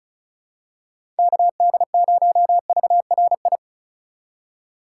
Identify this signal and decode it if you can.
Morse code, heard as a keyed tone, about 36 wpm, 700 Hz. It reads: KD0VRI
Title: USA Call Signs - 3x Character Spacing - 35wpm